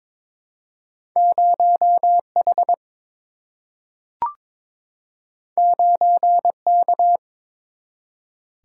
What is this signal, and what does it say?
Morse code, heard as a keyed tone, about 22 wpm, 700 Hz. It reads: 0H 9K
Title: Single-Letter Single-Number 22wpm